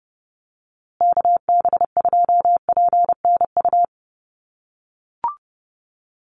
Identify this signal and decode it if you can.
Morse code, heard as a keyed tone, about 30 wpm, 700 Hz. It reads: KB2PNU
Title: USA Call Signs Repeated 2x 30wpm